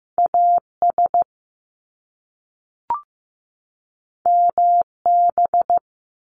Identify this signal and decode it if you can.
Morse code, heard as a keyed tone, about 15 wpm, 700 Hz. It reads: AS MB
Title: Top Two Letter Combinations in Words 15wpm